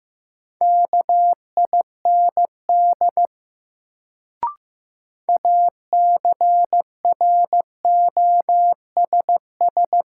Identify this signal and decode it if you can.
Morse code, heard as a keyed tone, about 15 wpm, 700 Hz. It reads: KIND ACROSS
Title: Sets of 2 Words - New Word Drill - 15wpm